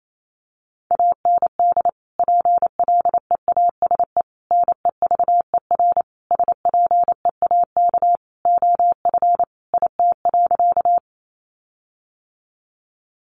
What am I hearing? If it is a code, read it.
Morse code, heard as a keyed tone, about 28 wpm, 700 Hz. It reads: AND PLEASE NEVER SPEAK OF IT.
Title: Sentences from Top 1500 Words 28wpm